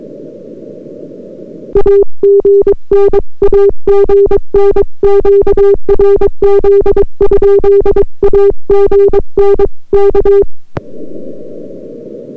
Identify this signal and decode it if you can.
Morse code, heard as a keyed tone, about 22 wpm, 400 Hz. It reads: AGN AGN QRZ? AGN K